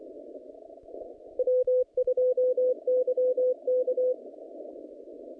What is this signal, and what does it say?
Morse code, heard as a keyed tone, about 24 wpm, 510 Hz. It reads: W2YK
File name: contests / naqp-cw-k5tr-op-k5pi- -0100